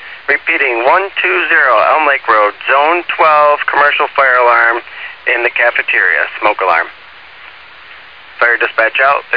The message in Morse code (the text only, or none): none